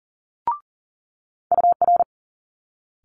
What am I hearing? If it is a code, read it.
Morse code, heard as a keyed tone, about 40 wpm, 700 Hz. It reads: UR
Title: Multi-letter Phonemes Quick Fire 40wpm